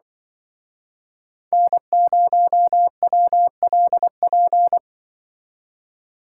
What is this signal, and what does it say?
Morse code, heard as a keyed tone, about 24 wpm, 700 Hz. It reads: N0WLP
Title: Speed Racing - USA 1x3 Call Signs 20wpm